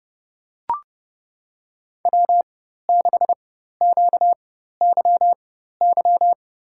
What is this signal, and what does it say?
Morse code, heard as a keyed tone, about 32 wpm, spaced as for 14 wpm, 700 Hz. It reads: W6QYY
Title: USA Call Signs - 5 Characters - 4x Character Spacing - 30wpm